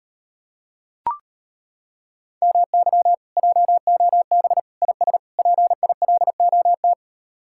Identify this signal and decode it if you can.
Morse code, heard as a keyed tone, about 32 wpm, 700 Hz. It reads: MY JOB IS PILOT